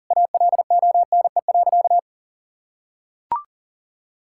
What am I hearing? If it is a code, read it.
Morse code, heard as a keyed tone, about 40 wpm, 700 Hz. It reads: ALONE.
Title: Sentences from Top 1000 Words - Review Part 6 - 40wpm